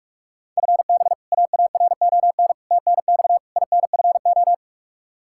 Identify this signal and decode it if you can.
Morse code, heard as a keyed tone, about 46 wpm, 700 Hz. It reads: FB AARON TNX INFO
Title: K1USN SST QSO Practice 45wpm